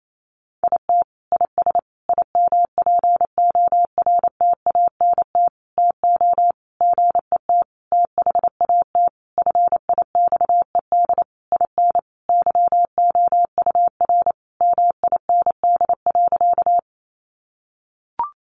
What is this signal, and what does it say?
Morse code, heard as a keyed tone, about 28 wpm, 700 Hz. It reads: IT IS IMPORTANT TO GET THAT FIXED IN YOUR MIND.
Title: Sentences from Top 1100 Words 28wpm